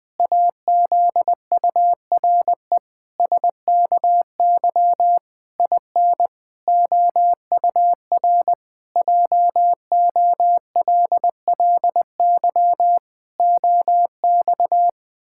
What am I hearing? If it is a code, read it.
Morse code, heard as a keyed tone, about 19 wpm, 700 Hz. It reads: AZURE SKY IN OUR JOLLY OX
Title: Pangram Challenge v2 - 20wpm